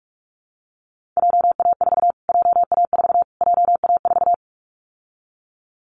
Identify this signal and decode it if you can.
Morse code, heard as a keyed tone, about 44 wpm, 700 Hz. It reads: JA4 JA4 JA4
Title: Callsign Prefixes for Top 25 DXCC - Repeated 3x - 45wpm